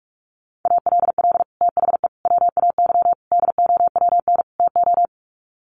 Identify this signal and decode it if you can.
Morse code, heard as a keyed tone, about 44 wpm, 700 Hz. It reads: ALL THE WAY DOWN TO
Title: Sets of 5 Words Encore2 45wpm